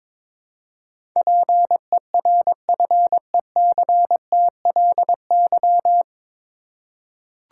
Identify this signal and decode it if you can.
Morse code, heard as a keyed tone, about 22 wpm, 700 Hz. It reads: PERFECTLY
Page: Morse Code Ninja practice 22wpm